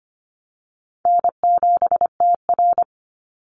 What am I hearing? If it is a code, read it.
Morse code, heard as a keyed tone, about 25 wpm, 700 Hz. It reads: N7TR
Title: Easy Call Signs - 25wpm